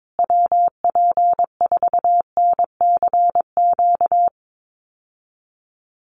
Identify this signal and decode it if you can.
Morse code, heard as a keyed tone, about 22 wpm, 700 Hz. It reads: WP4NCQ
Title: USA Call Signs Repeated 2x 22wpm